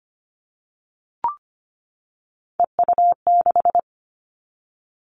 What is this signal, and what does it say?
Morse code, heard as a keyed tone, about 25 wpm, 700 Hz. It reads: EU6
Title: Callsign Prefixes for Top 100 DXCC - 25wpm